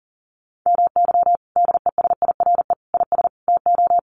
Morse code, heard as a keyed tone, about 40 wpm, 700 Hz. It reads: MY DESIRE IS TO
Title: Sentences from Top 1100 Words 40wpm